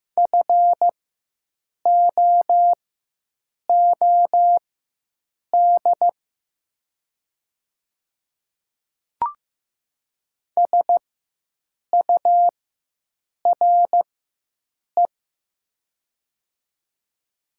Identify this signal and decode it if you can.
Morse code, heard as a keyed tone, about 15 wpm, spaced as for 7 wpm, 700 Hz. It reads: FOOD SURE
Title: 4x Character Spacing - Top 300 Words - 15wpm